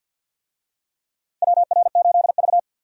Morse code, heard as a keyed tone, about 52 wpm, 700 Hz. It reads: WR8V
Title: USA Call Signs - 4 Characters - 50wpm